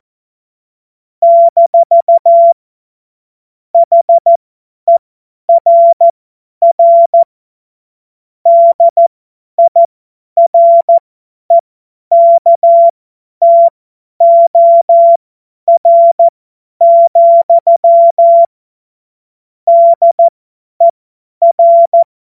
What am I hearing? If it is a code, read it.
Morse code, heard as a keyed tone, about 14 wpm, spaced as for 10 wpm, 680 Hz. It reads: - HERR DIREKTOR, DER
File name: Morsekurs / Witze / Witz 29 April 2025 50 F70